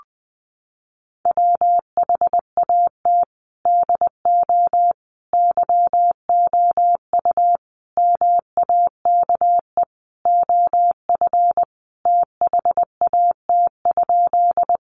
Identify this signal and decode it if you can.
Morse code, heard as a keyed tone, about 20 wpm, 700 Hz. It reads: WHAT DO YOU MAKE OF THAT?
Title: Sentences from Top 100 Words 20wpm